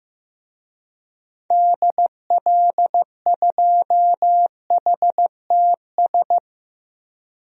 Morse code, heard as a keyed tone, about 15 wpm, 700 Hz. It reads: DL2HTS
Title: DXCC-100 Call Signs - 6 Characters - Repeated 3x - 15wpm